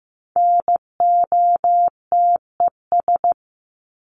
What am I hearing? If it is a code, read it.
Morse code, heard as a keyed tone, about 15 wpm, 700 Hz. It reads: NOTES
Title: ES:S Suffix - Top 100 Words with Base Words 15wpm